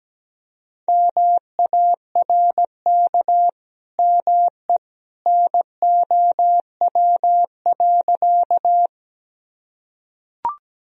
Morse code, heard as a keyed tone, about 17 wpm, 700 Hz. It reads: MARK ME NOW.
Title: Sentences from Top 1000 Words - Review Part 1 - 17wpm